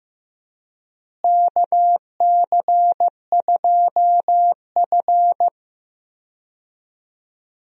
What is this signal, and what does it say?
Morse code, heard as a keyed tone, about 16 wpm, 700 Hz. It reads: KC2F
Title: USA Call Signs - 4 Characters - Repeated 2x - 15wpm